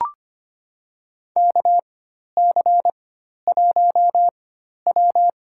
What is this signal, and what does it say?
Morse code, heard as a keyed tone, about 25 wpm, spaced as for 12 wpm, 700 Hz. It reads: KC1W
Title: USA Call Signs - 4 Characters - 4x Character Spacing - 25wpm